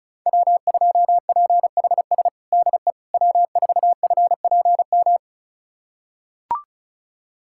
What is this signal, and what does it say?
Morse code, heard as a keyed tone, about 36 wpm, 700 Hz. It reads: W2PHS DE W4FPM
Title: USA Callsign DE Callsign - 5 Characters - 35wpm